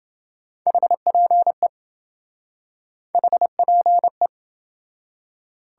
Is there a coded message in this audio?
Yes: Morse code from 0.7 to 4.3 s, heard as a keyed tone, about 28 wpm, 700 Hz. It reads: HPE HPE